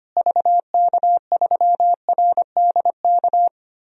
Morse code, heard as a keyed tone, about 25 wpm, 700 Hz. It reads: VK3RDK
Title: DXCC-100 Call Signs - 6 Characters - Repeated 2x - 25wpm